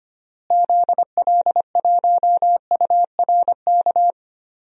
Morse code, heard as a keyed tone, about 25 wpm, 700 Hz. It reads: ZL1URK
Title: DXCC-100 Call Signs - 6 Characters - Repeated 3x - 25wpm